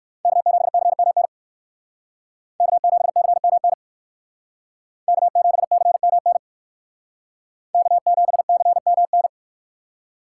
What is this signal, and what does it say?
Morse code, heard as a keyed tone, about 58 wpm, 700 Hz. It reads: K7CMN K7CMN K7CMN K7CMN